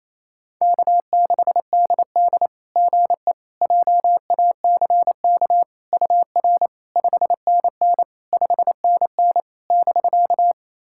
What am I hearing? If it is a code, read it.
Morse code, heard as a keyed tone, about 28 wpm, 700 Hz. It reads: K6DD GE JACK UR 5NN 5NN <BK>
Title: SOTA QSO Practice 28wpm